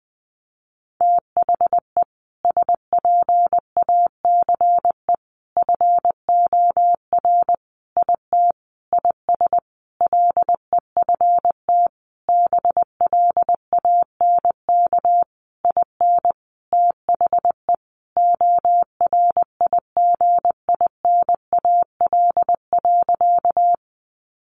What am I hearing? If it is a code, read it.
Morse code, heard as a keyed tone, about 20 wpm, 700 Hz. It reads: THE SPACE FOR IT IS LEFT BLANK IN THE ORIGINAL.
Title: Sentences from Top 1500 Words 20wpm